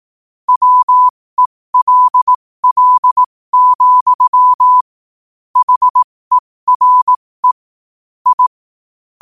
Morse code, heard as a keyed tone, about 18 wpm, spaced as for 15 wpm, 1000 Hz. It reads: WELL, HERE I
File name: qotdfiles / 15wpm